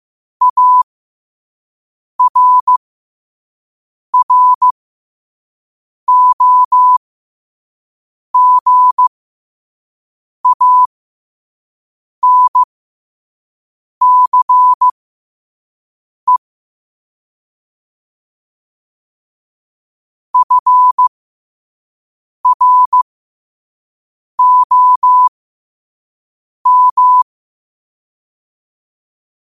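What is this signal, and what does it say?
Morse code, heard as a keyed tone, about 15 wpm, spaced as for 5 wpm, 1000 Hz. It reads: ARROGANCE FROM
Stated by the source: Morse sent at 5 wpm.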